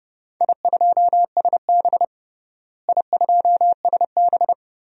Morse code, heard as a keyed tone, about 30 wpm, 700 Hz. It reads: I2SB I2SB
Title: DXCC-100 Call Signs - 4 Characters - Repeated 3x - 30wpm